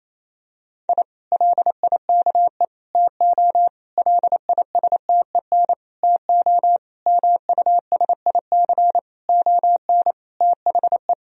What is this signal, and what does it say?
Morse code, heard as a keyed tone, about 28 wpm, 700 Hz. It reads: I LIKE TO LISTEN TO MUSIC ON THE
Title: Sentences from Top 1000 Words - Review Part 3 - 28wpm